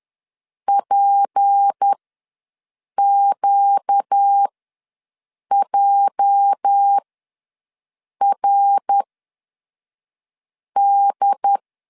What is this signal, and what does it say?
Morse code, heard as a keyed tone, about 10 wpm, spaced as for 5 wpm, 800 Hz. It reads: PQJRD
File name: LEZIONI / 9 28 ws 10wpm 4124